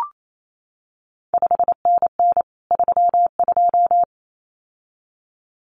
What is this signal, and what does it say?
Morse code, heard as a keyed tone, about 28 wpm, 700 Hz. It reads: 5NN 32
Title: CQ WW DX Contest Exchange Encore - 28wpm